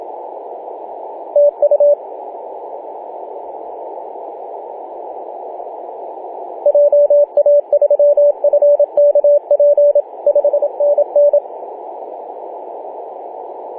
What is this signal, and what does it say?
Morse code, heard as a keyed tone, about 26 wpm, 580 Hz, with radio noise. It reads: TU JA3FKP 5NN
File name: posts / oct-08-2033z-tx5z-21-008mhz-cw